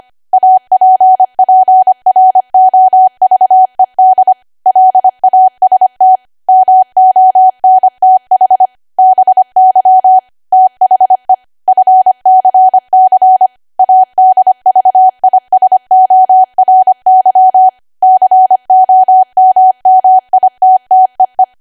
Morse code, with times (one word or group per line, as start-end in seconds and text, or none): 0.3-4.3 APPROVED
4.7-6.2 LAST
6.5-8.7 MONTH
9.0-10.2 BY
10.5-11.3 THE
11.7-13.5 FCC
13.8-17.7 ADVISORY
18.0-21.4 COMMITTEE